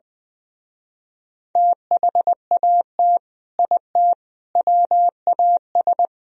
Morse code, heard as a keyed tone, about 24 wpm, 700 Hz. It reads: THAT IT WAS
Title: Speed-Racing - Sets of 3 Words 20wpm